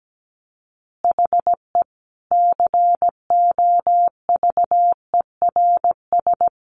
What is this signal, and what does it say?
Morse code, heard as a keyed tone, about 17 wpm, 700 Hz. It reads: HE COVERS
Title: Easy Phrases 17wpm